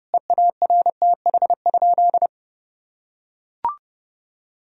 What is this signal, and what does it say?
Morse code, heard as a keyed tone, about 32 wpm, 700 Hz. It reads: EARTH?
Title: Sentences from Top 1000 Words - Review Part 3 - 30wpm